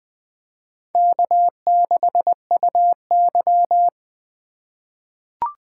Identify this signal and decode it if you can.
Morse code, heard as a keyed tone, about 20 wpm, 700 Hz. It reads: K6UY
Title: USA Call Signs - 4 Characters - 3x Character Spacing - 20wpm